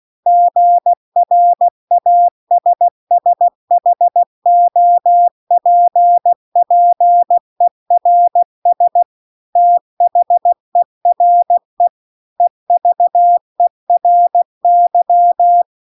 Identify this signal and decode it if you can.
Morse code, heard as a keyed tone, about 16 wpm, 700 Hz. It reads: GRASSHOPPERS THERE EVERY
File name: CW-The-Big-Twoheaded-River-2-16WPM